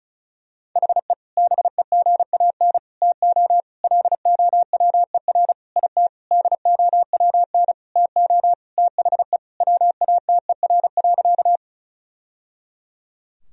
Morse code, heard as a keyed tone, about 34 wpm, 700 Hz. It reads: HE BEGAN TO LOWER IT DOWN TO THE WATER.
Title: Sentences from Top 1000 Words - Review Part 4 - 35wpm